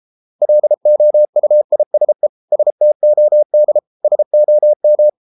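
Morse code, heard as a keyed tone, about 33 wpm, 600 Hz. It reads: LOUISE STOD SOM
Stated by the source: Morse code at 33 wpm, QRQ.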